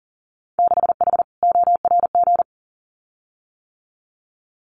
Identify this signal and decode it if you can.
Morse code, heard as a keyed tone, about 40 wpm, 700 Hz. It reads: 6H ORG
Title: Winter Field Day Exchanges Encore 40wpm